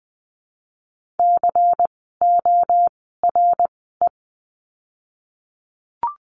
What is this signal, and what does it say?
Morse code, heard as a keyed tone, about 20 wpm, spaced as for 15 wpm, 700 Hz. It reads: CORE
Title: 094 Lesson 16f - Letter c - Words 20wpm